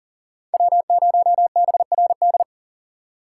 Morse code, heard as a keyed tone, about 53 wpm, spaced as for 47 wpm, 700 Hz. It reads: W0BRD
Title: Speed Racing - USA 1x3 Call Signs 40wpm